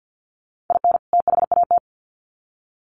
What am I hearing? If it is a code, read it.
Morse code, heard as a keyed tone, about 50 wpm, 700 Hz. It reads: IN THAT
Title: Sets of 2 Words 50wpm